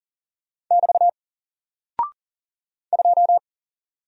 Morse code, heard as a keyed tone, about 39 wpm, 700 Hz. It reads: <BT> 2
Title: Single Character (Letter:Number:Punctuation:Prosign) 40wpm